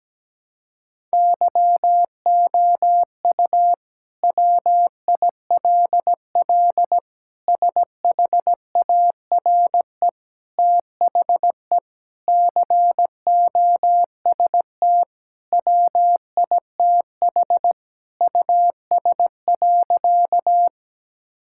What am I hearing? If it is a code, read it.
Morse code, heard as a keyed tone, about 17 wpm, 700 Hz. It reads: YOU WILL SHARE THE COST WITH US.
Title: Sentences from Top 1000 Words - Review Part 6 - 17wpm